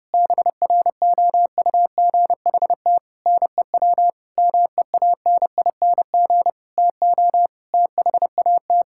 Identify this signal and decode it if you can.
Morse code, heard as a keyed tone, about 30 wpm, 700 Hz. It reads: BROUGHT NEW MEANING TO THAT
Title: Sentences from Top 1600 Words 30wpm